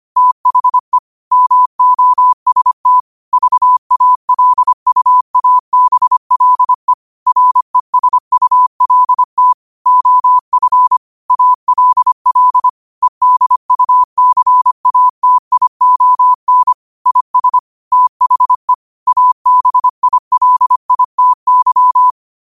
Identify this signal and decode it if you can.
Morse code, heard as a keyed tone, about 25 wpm, 1000 Hz. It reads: THE MOST VALUABLE RESULT OF ALL EDUCATION IS THE ABILITY
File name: qotdfiles / 25wpm